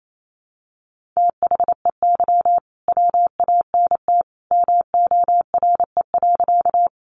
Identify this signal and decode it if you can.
Morse code, heard as a keyed tone, about 28 wpm, 700 Hz. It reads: THEY WANT MORE.
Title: Sentences from Top 200 Words 28wpm